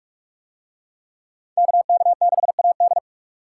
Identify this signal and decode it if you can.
Morse code, heard as a keyed tone, about 45 wpm, 700 Hz. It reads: KK6AD